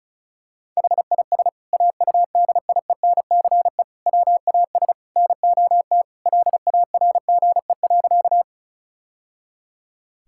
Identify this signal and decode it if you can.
Morse code, heard as a keyed tone, about 36 wpm, 700 Hz. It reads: HIS AUDIENCE WAS NOT LARGE.
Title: Sentences from Top 1000 Words - Review Part 5 - 35wpm